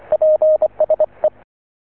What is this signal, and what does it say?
Morse code, heard as a keyed tone, about 25 wpm, 620 Hz. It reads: PSE